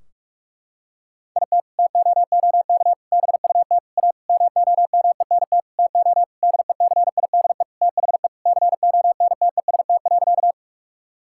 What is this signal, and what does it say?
Morse code, heard as a keyed tone, about 42 wpm, 700 Hz. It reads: IT TOOK BUT A MOMENT TO DECIDE THE CONTEST.
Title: Sentences from Top 1600 Words 45wpm